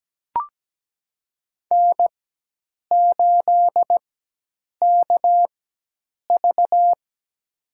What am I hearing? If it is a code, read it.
Morse code, heard as a keyed tone, about 17 wpm, spaced as for 8 wpm, 700 Hz. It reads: N8KV
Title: USA Call Signs - 4x Character Spacing - 17wpm